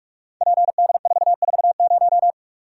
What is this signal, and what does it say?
Morse code, heard as a keyed tone, about 46 wpm, 700 Hz. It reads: PD440
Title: Code Groups - 5 Letter:Numbers - 45wpm